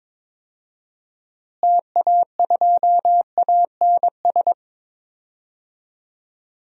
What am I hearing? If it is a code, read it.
Morse code, heard as a keyed tone, about 22 wpm, 700 Hz. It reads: TA2ANS
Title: DXCC-100 Call Signs - 6 Characters - Repeated 3x - 22wpm